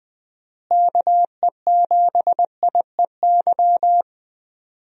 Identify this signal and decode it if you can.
Morse code, heard as a keyed tone, about 20 wpm, 700 Hz. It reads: KE7IEY
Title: USA Call Signs - 6 Characters - Repeated 3x - 20wpm